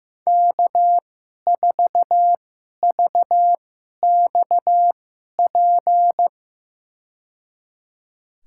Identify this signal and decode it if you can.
Morse code, heard as a keyed tone, about 16 wpm, spaced as for 11 wpm, 700 Hz. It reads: K4VXP
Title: USA Call Signs - 5 Characters - 2x Character Spacing - 15wpm